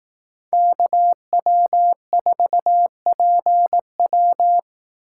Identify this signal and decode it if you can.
Morse code, heard as a keyed tone, about 18 wpm, 700 Hz. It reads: KW4PW
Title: Speed Racing - USA 2x2 Call Signs 15wpm